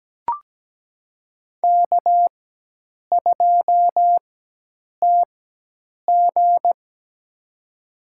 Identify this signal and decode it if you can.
Morse code, heard as a keyed tone, about 17 wpm, spaced as for 8 wpm, 700 Hz. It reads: K2TG
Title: USA Call Signs - 4 Characters - 4x Character Spacing - 17wpm